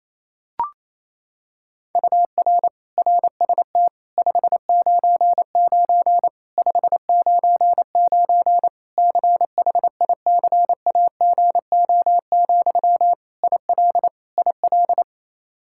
Morse code, heard as a keyed tone, about 28 wpm, 700 Hz. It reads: UR RST 599 599 CHICAGO, IL IL
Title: Practice QSO - Exchange 1 - Beginning - 28wpm